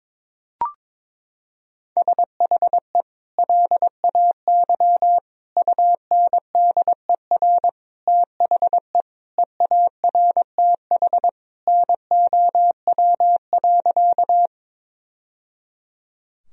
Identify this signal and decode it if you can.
Morse code, heard as a keyed tone, about 22 wpm, 700 Hz. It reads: SHE LAY UNDER THE EARTH NOW.
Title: Sentences from Top 1000 Words - Review Part 2 - 22wpm